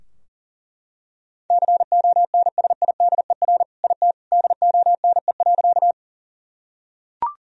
Morse code, heard as a keyed tone, about 40 wpm, 700 Hz. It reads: CONSIDER IT DONE.
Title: Sentences from Top 1000 Words - Review Part 4 - 40wpm